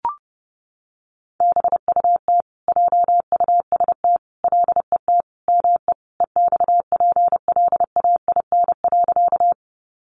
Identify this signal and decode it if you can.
Morse code, heard as a keyed tone, about 30 wpm, 700 Hz. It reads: BUT JUST LET ME EXPLAIN.
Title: Sentences from Top 1100 Words 30wpm